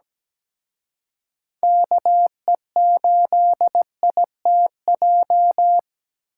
Morse code, heard as a keyed tone, about 20 wpm, 700 Hz. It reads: KE8ITJ
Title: Speed Racing - USA 2x3 Call Signs 17wpm